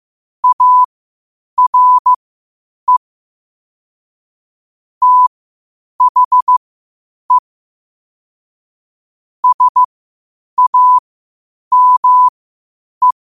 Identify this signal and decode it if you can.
Morse code, heard as a keyed tone, about 15 wpm, spaced as for 8 wpm, 1000 Hz. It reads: ARE THE SAME
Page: New quotes every day in morse code at 8 Words per minute.